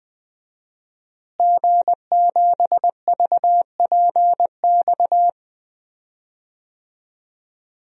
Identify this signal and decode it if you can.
Morse code, heard as a keyed tone, about 20 wpm, 700 Hz. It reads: G7VPX
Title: DXCC-100 Call Signs - 5 Characters - Repeated 3x - 20wpm